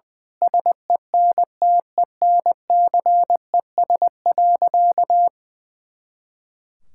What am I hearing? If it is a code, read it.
Morse code, heard as a keyed tone, about 20 wpm, 700 Hz. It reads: SENTENCES.
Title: Pangram Challenge v2 - 20wpm